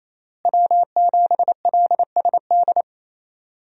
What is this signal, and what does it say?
Morse code, heard as a keyed tone, about 28 wpm, 700 Hz. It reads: W7LSD
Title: USA Call Signs - 5 Characters - Repeated 3x - 28wpm